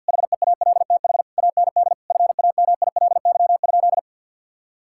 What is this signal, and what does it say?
Morse code, heard as a keyed tone, about 50 wpm, 700 Hz. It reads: HEALTH AND FAMILY?
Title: Sentences from Top 1600 Words 50wpm